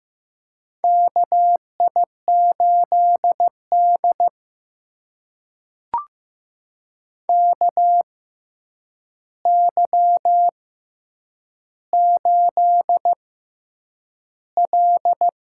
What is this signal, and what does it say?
Morse code, heard as a keyed tone, about 15 wpm, spaced as for 8 wpm, 700 Hz. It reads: KI8D KY8L
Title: USA Call Signs - 4 Characters - 6x Character Spacing - 15wpm